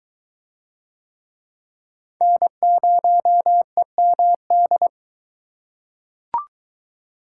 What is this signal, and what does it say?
Morse code, heard as a keyed tone, about 22 wpm, 700 Hz. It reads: N0EMD